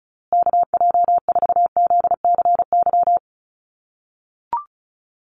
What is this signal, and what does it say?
Morse code, heard as a keyed tone, about 34 wpm, 700 Hz. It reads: KJ4ZCY
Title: USA Call Signs Repeated 2x 35wpm